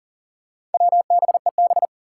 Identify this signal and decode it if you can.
Morse code, heard as a keyed tone, about 40 wpm, 700 Hz. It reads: WBEB
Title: Four Challenging Letter-Number Combinations 40wpm